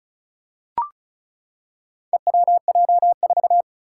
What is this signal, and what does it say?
Morse code, heard as a keyed tone, about 36 wpm, 700 Hz. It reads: EWJ4
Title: Four Challenging Letter-Number Combinations 35wpm